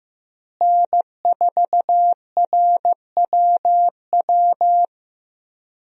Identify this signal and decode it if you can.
Morse code, heard as a keyed tone, about 15 wpm, 700 Hz. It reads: N4RWW
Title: USA Call Signs - 5 Characters - Repeated 2x - 15wpm